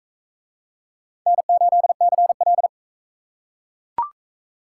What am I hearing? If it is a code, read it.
Morse code, heard as a keyed tone, about 42 wpm, 700 Hz. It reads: N8CL